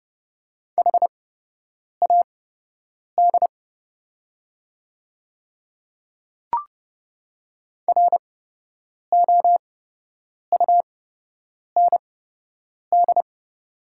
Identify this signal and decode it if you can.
Morse code, heard as a keyed tone, about 32 wpm, spaced as for 8 wpm, 700 Hz. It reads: HAD ROUND
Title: 8x Character Spacing - Top 200 Words - 30wpm